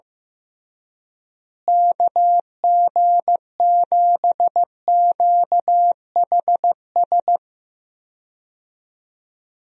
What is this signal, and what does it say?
Morse code, heard as a keyed tone, about 15 wpm, 700 Hz. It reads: KG7QHS
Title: USA Call Signs - 6 Characters - Repeated 2x - 15wpm